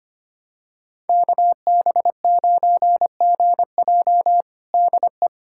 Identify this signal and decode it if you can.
Morse code, heard as a keyed tone, about 25 wpm, 700 Hz. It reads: KB9GJ DE
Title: USA Callsign DE Callsign - 5 Characters - 25wpm